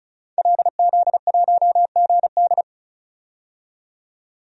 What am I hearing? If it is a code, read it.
Morse code, heard as a keyed tone, about 34 wpm, 700 Hz. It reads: LZ1GD
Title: DXCC-100 Call Signs - 5 Characters - 35wpm